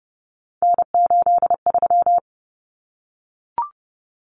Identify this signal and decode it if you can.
Morse code, heard as a keyed tone, about 30 wpm, 700 Hz. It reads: N83
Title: Common Cut Numbers - 3-Digit 30wpm